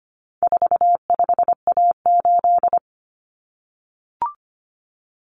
Morse code, heard as a keyed tone, about 25 wpm, 700 Hz. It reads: 45A8
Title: Common Cut Numbers - 4-Digit 25wpm